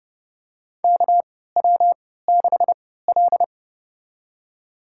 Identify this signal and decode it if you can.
Morse code, heard as a keyed tone, about 30 wpm, spaced as for 17 wpm, 700 Hz. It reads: KW6L
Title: USA Call Signs - 4 Characters - 3x Character Spacing - 30wpm